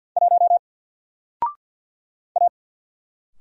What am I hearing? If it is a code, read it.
Morse code, heard as a keyed tone, about 53 wpm, 700 Hz. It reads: 1 A
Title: Single Letter-Number Alt 50wpm